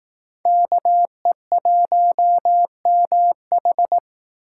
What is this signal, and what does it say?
Morse code, heard as a keyed tone, about 18 wpm, 700 Hz. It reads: KE1MH
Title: Speed Racing - USA 2x2 Call Signs 15wpm